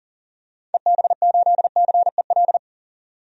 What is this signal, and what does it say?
Morse code, heard as a keyed tone, about 40 wpm, 700 Hz. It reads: EB8CEL
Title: DXCC-100 Call Signs - 6 Characters - 40wpm